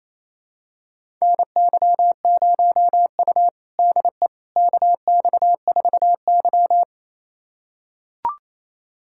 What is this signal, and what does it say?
Morse code, heard as a keyed tone, about 28 wpm, 700 Hz. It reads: NY0U DE KX4Y
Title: USA Callsign DE Callsign - 4 Characters - 28wpm